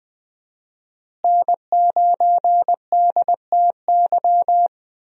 Morse code, heard as a keyed tone, about 20 wpm, 700 Hz. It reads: N9DTY
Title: USA Call Signs - 5 Characters - Repeated 3x - 20wpm